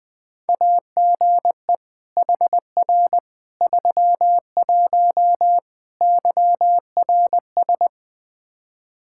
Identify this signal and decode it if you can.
Morse code, heard as a keyed tone, about 20 wpm, 700 Hz. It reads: AGE HR 31 YRS
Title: Practice QSO - Exchange 3 - More Info - 20wpm